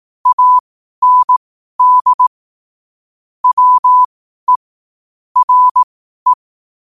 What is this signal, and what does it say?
Morse code, heard as a keyed tone, about 18 wpm, spaced as for 12 wpm, 1000 Hz. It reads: AND WE RE
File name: qotdfiles / 12wpm